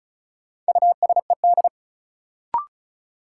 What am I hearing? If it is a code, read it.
Morse code, heard as a keyed tone, about 36 wpm, 700 Hz. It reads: USED
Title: Sentences from Top 600 Words - New Word Drill - 35wpm